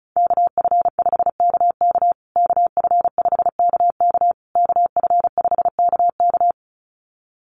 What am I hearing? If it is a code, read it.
Morse code, heard as a keyed tone, about 34 wpm, 700 Hz. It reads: KF5KK KF5KK KF5KK
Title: USA Call Signs Repeated 3x 35wpm